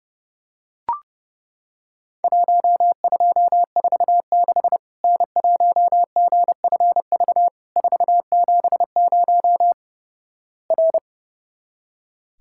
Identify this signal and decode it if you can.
Morse code, heard as a keyed tone, about 30 wpm, 700 Hz. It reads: 1246 N1GFV 470 R
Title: Worked All Europe DX Contest - QTC Exchange - 30wpm